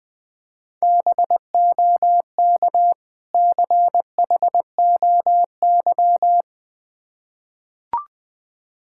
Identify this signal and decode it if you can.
Morse code, heard as a keyed tone, about 20 wpm, 700 Hz. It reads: BOK CHOY
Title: Difficult Words with Hints - 20wpm